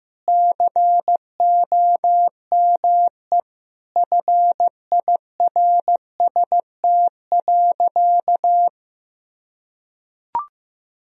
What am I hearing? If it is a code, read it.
Morse code, heard as a keyed tone, about 15 wpm, 700 Hz. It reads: COME FIRST.
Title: Sentences from Top 1000 Words - Review Part 4 - 15wpm